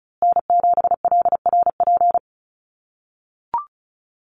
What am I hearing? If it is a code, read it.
Morse code, heard as a keyed tone, about 34 wpm, 700 Hz. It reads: N7LRP
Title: USA Call Signs Repeated 2x 35wpm